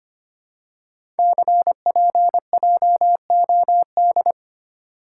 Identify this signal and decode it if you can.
Morse code, heard as a keyed tone, about 25 wpm, 700 Hz. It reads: CPJOD
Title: Code Groups - 5 Letters - 25wpm